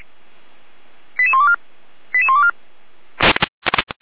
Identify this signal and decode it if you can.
Five-tone selective call (ZVEI), heard as quick runs of five tones, tones of about 70 ms. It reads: 88124 88124